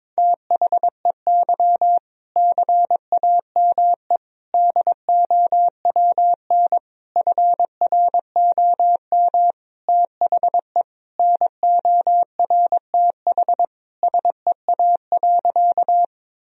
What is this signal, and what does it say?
Morse code, heard as a keyed tone, about 22 wpm, 700 Hz. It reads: THEY CAME DOWN FROM THE NORTH SEA.
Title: Sentences from Top 1000 Words - Review Part 1 - 22wpm